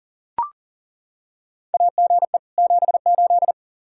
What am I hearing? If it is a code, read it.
Morse code, heard as a keyed tone, about 37 wpm, 700 Hz. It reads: AGE 78
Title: K1USN SST QSO Practice 40wpm